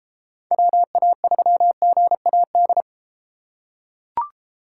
Morse code, heard as a keyed tone, about 33 wpm, 700 Hz. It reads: WA3GAD